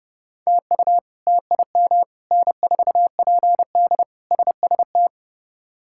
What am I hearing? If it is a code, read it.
Morse code, heard as a keyed tone, about 30 wpm, 700 Hz. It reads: TU TIM N4PD SST
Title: K1USN SST QSO Practice - 30wpm